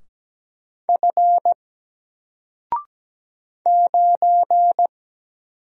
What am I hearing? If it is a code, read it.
Morse code, heard as a keyed tone, about 17 wpm, 700 Hz. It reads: F 9
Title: Single Character (Letter:Number:Punctuation:Prosign) 17wpm